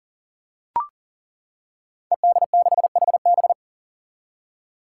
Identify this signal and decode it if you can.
Morse code, heard as a keyed tone, about 40 wpm, 700 Hz. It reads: ED6HB
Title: Five Challenging Letter-Number Combinations 40wpm